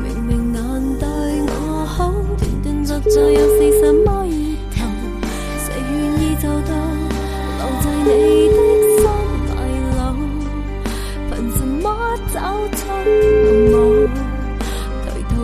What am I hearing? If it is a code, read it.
Ringback tone (Europe, 425 Hz), with noise.